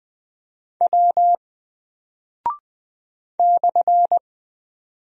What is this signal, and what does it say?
Morse code, heard as a keyed tone, about 20 wpm, 700 Hz. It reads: W /
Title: Single Character (Letter:Number:Punctuation:Prosign) Alt 20wpm